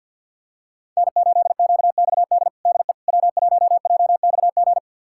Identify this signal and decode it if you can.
Morse code, heard as a keyed tone, about 47 wpm, 700 Hz. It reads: N8QXD DE W1JXG
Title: USA Callsign DE Callsign 50wpm